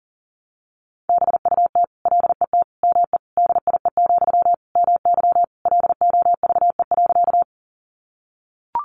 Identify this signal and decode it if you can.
Morse code, heard as a keyed tone, about 40 wpm, 700 Hz. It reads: BUT LET ME DIE, MY LOVE.
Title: Sentences from Top 1300 Words 40wpm